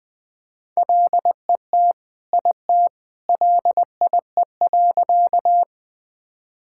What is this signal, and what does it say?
Morse code, heard as a keyed tone, about 20 wpm, 700 Hz. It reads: LET IT LIE.
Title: Sentences from Top 1600 Words 20wpm